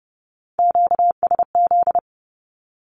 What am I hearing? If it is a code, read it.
Morse code, heard as a keyed tone, about 30 wpm, 700 Hz. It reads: QSZ
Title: Q-Codes 30wpm